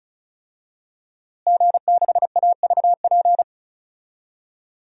Morse code, heard as a keyed tone, about 34 wpm, 700 Hz. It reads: G6AVP
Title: DXCC-100 Call Signs - 5 Characters - Repeated 3x - 35wpm